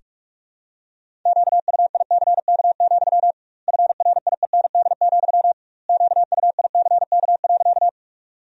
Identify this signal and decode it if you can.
Morse code, heard as a keyed tone, about 46 wpm, 700 Hz. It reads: QUICK, FRIEND, QUICK.
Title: 2x Word-Spacing - Sentences from Top 500 Words - 45wpm